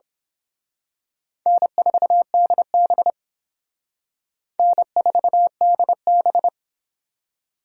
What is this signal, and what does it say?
Morse code, heard as a keyed tone, about 26 wpm, 700 Hz. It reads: N4DB N4DB
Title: Speed Racing - USA 1x2 Call Signs 22wpm